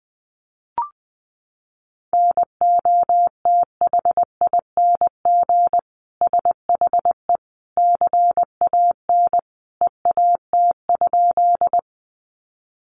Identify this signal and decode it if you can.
Morse code, heard as a keyed tone, about 20 wpm, 700 Hz. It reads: NOTHING SHE CAN EAT?
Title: Sentences from Top 500 Words 20wpm